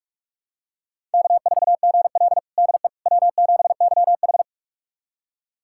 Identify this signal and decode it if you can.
Morse code, heard as a keyed tone, about 42 wpm, 700 Hz. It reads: K4GL DE W7YH
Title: USA Callsign DE Callsign - 4 Characters - 45wpm